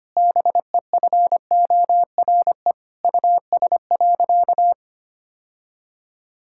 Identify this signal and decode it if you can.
Morse code, heard as a keyed tone, about 25 wpm, 700 Hz. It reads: BEFORE US.
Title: Sentences from Top 1000 Words - Review Part 5 - 25wpm